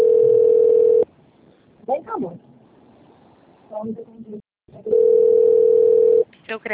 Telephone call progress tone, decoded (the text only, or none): ringback tone